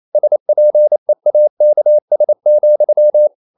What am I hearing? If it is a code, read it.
Morse code, heard as a keyed tone, about 28 wpm, 600 Hz. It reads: SPEAKS,
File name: CW / Practice / EnglishQuotations0228